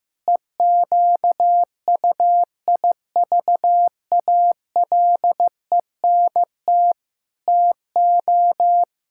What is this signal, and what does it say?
Morse code, heard as a keyed tone, about 15 wpm, 700 Hz. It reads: EQUIVALENT TO
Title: Sentences from Top 1600 Words 15wpm